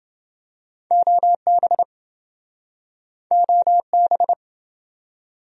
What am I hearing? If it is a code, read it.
Morse code, heard as a keyed tone, about 28 wpm, 700 Hz. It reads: OB OB